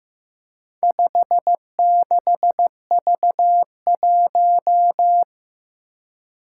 Morse code, heard as a keyed tone, about 15 wpm, 700 Hz. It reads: 56V1
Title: Four Challenging Letter-Number Combinations 15wpm